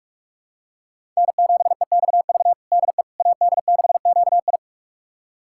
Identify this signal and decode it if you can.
Morse code, heard as a keyed tone, about 46 wpm, 700 Hz. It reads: N7EXV DE AD6QI
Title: USA Callsign DE Callsign - 5 Characters - 45wpm